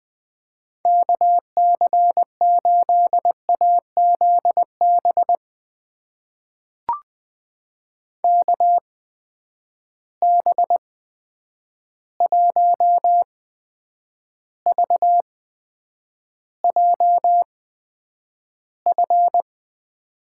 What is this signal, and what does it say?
Morse code, heard as a keyed tone, about 20 wpm, spaced as for 9 wpm, 700 Hz. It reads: KC8AZB KB1VJF
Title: USA Call Signs - 6 Characters - 8x Character Spacing - 20wpm